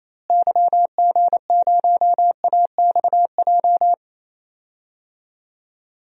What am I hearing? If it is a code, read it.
Morse code, heard as a keyed tone, about 28 wpm, 700 Hz. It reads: YG0AXJ
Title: DXCC-100 Call Signs - 6 Characters - Repeated 3x - 28wpm